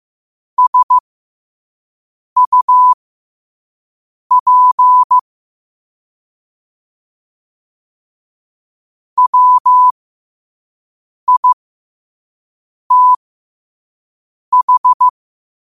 Morse code, heard as a keyed tone, about 15 wpm, spaced as for 5 wpm, 1000 Hz. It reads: SUP WITH
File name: qotdfiles / 5wpm